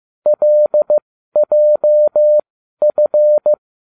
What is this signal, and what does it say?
Morse code, heard as a keyed tone, about 15 wpm, spaced as for 12 wpm, 600 Hz. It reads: LJF